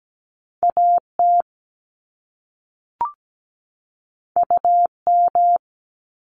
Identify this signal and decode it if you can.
Morse code, heard as a keyed tone, about 17 wpm, 700 Hz. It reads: AT UM
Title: Top Two Letter Combinations in Words 17wpm